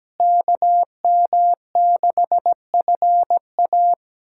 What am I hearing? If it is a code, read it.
Morse code, heard as a keyed tone, about 17 wpm, 700 Hz. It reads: KM6FA
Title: USA Callsign DE Callsign - 5 Characters - 17wpm